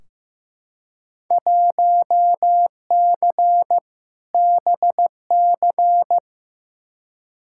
Morse code, heard as a keyed tone, about 15 wpm, 700 Hz. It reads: 1C BC
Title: Field Day Exchanges Encore - 15wpm